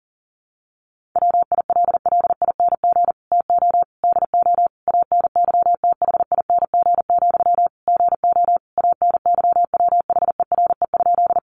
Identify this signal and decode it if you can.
Morse code, heard as a keyed tone, about 40 wpm, 700 Hz. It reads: WILLING TO DO ANYTHING, GO ANYWHERE?
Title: Sentences from Top 1400 Words 40wpm